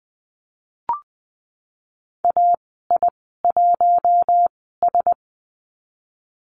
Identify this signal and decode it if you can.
Morse code, heard as a keyed tone, about 20 wpm, spaced as for 15 wpm, 700 Hz. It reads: AI1S
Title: 062 Lesson 11g - Letter h - Call Signs 20wpm